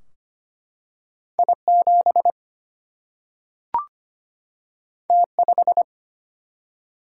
Morse code, heard as a keyed tone, about 25 wpm, 700 Hz. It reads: I7 T5
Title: Single-Letter Single-Number 25wpm